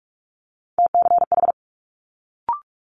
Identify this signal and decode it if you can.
Morse code, heard as a keyed tone, about 46 wpm, 700 Hz. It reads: TCH
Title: Multi-letter Phonemes Quick Fire 45wpm